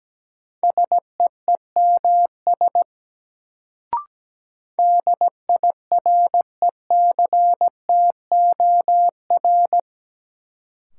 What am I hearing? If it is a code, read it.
Morse code, heard as a keyed tone, about 17 wpm, 700 Hz. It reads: SEEMS DIRECTOR
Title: Sets of 3 Words - New Word Drill - 17wpm